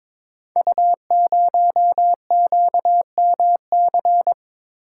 Morse code, heard as a keyed tone, about 22 wpm, 700 Hz. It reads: U0QMC
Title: Code Groups - 5 Letter:Numbers - 22wpm